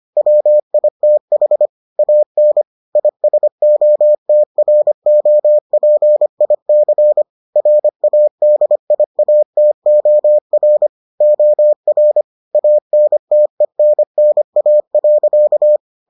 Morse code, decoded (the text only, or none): WITH AN ISOTROPIC RADIATOR OR ANTENNA.